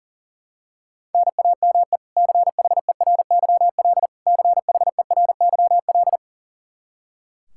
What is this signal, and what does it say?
Morse code, heard as a keyed tone, about 40 wpm, 700 Hz. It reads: NAME CHERYL CHERYL
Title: K1USN SST QSO Practice 40wpm